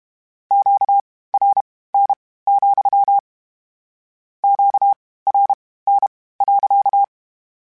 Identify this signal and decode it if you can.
Morse code, heard as a keyed tone, about 32 wpm, spaced as for 18 wpm, 800 Hz. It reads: QRN, QRN.
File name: Q Calls / Q Calls-mp3s / Q Calls sppc @30